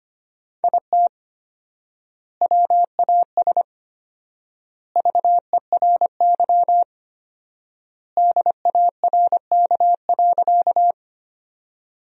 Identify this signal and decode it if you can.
Morse code, heard as a keyed tone, about 25 wpm, 700 Hz. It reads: IT WAS VERY DARK.
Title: 4x Word-Spacing - Sentences from Top 500 Words - 25wpm